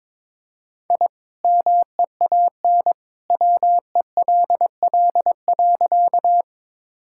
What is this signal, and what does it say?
Morse code, heard as a keyed tone, about 22 wpm, 700 Hz. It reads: I MEAN WELL.
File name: practice / 22wpm / Sentences with 2-Word Phrases - 22wpm